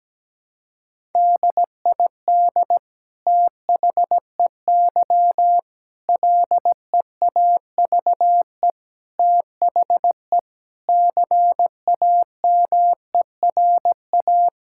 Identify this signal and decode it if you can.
Morse code, heard as a keyed tone, about 17 wpm, 700 Hz. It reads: DID THEY LEAVE THE CAMERA
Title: Sentences from Top 1000 Words - Review Part 5 - 17wpm